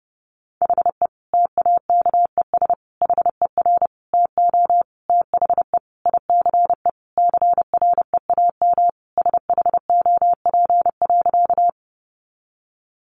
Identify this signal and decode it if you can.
Morse code, heard as a keyed tone, about 30 wpm, 700 Hz. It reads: HE TAKES HER TO THE ICE CREAM SHOP.
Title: Additional Easy Phrases 30wpm